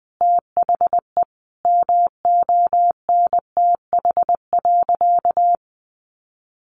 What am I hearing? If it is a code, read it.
Morse code, heard as a keyed tone, about 20 wpm, 700 Hz. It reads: THE MONTH.
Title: Sentences from Top 800 Words 20wpm